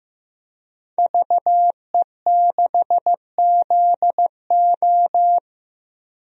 Morse code, heard as a keyed tone, about 15 wpm, 700 Hz. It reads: VE6ZO
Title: DXCC-100 Call Signs - 5 Characters - Repeated 2x - 15wpm